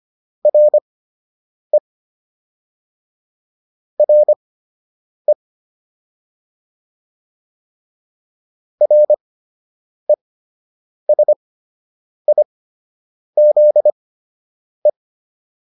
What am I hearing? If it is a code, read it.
Morse code, heard as a keyed tone, about 25 wpm, spaced as for 8 wpm, 600 Hz. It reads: RE RE RESIZE